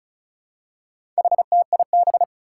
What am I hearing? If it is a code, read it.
Morse code, heard as a keyed tone, about 36 wpm, 700 Hz. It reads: HTIB
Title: Four Challenging Letter-Number Combinations 35wpm